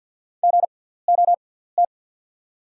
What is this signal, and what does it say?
Morse code, heard as a keyed tone, about 50 wpm, spaced as for 17 wpm, 690 Hz. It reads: GOT
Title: 6x Character Spacing - Top 300 Words - 50wpm